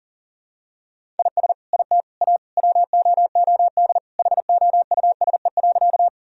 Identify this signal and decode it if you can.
Morse code, heard as a keyed tone, about 37 wpm, 700 Hz. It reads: IS IT A WOOD HOUSE.
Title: Sentences from Top 1000 Words - Review Part 2 - 40wpm